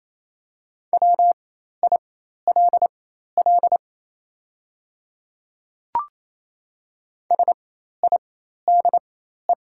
Morse code, heard as a keyed tone, about 27 wpm, spaced as for 13 wpm, 700 Hz. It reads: WILL SIDE
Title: 4x Character Spacing - Top 100 Words - 28wpm